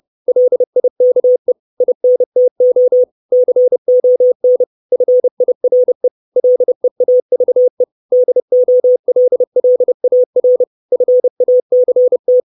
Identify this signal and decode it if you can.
Morse code, heard as a keyed tone, about 30 wpm, 500 Hz. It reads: LIKE INTO CON FIRE LEAVE DOLLAR FACT
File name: 632 Words 30WPM